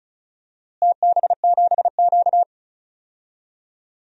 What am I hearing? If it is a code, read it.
Morse code, heard as a keyed tone, about 34 wpm, 700 Hz. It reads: TB7Q
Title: DXCC-100 Call Signs - 4 Characters - Repeated 2x - 35wpm